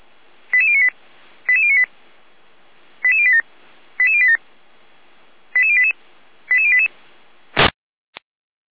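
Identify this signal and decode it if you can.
Five-tone selective call (ZVEI), heard as quick runs of five tones, tones of about 70 ms. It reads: 88098 88098 88087 88087 88088 88088
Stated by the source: Five-tone ZVEI selective call received by SDR.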